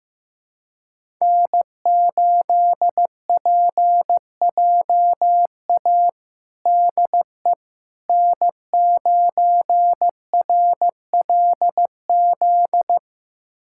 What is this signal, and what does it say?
Morse code, heard as a keyed tone, about 16 wpm, 700 Hz. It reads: N8PJA DE N9RLZ
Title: USA Callsign DE Callsign - 5 Characters - 15wpm